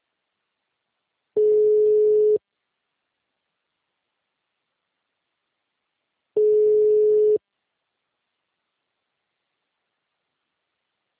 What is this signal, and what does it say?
Ringback tone (Europe, 425 Hz).